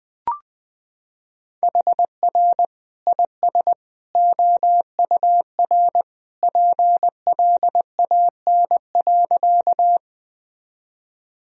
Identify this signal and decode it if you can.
Morse code, heard as a keyed tone, about 20 wpm, 700 Hz. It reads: HR IS OUR PLAN.
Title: Sentences from Top 500 Words and Abbreviations 20wpm